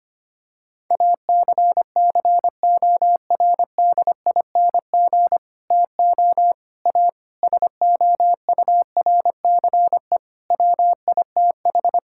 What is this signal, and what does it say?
Morse code, heard as a keyed tone, about 25 wpm, 700 Hz. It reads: ACCORDING TO A SOURCE WITH
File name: practice / 25wpm / Sentences with 9-Word Phrases - 25wpm